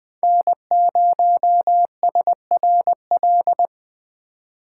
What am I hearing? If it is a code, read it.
Morse code, heard as a keyed tone, about 20 wpm, 700 Hz. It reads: N0SRL
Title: USA Call Signs - 5 Characters - 2x Character Spacing - 20wpm